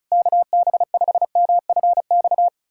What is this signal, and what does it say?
Morse code, heard as a keyed tone, about 36 wpm, 700 Hz. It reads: KB5MFX
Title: USA Callsign DE Callsign - 6 Characters - 35wpm